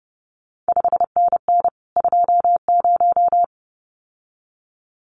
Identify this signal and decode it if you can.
Morse code, heard as a keyed tone, about 30 wpm, 700 Hz. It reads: 5NN 20
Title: CQ WW DX Contest Exchange Encore - 30wpm